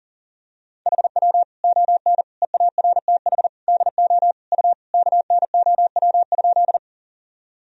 Morse code, heard as a keyed tone, about 40 wpm, 700 Hz. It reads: HW ON EARTH DO U KNOW?
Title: Sentences from Top 200 Words and Abbreviations 40wpm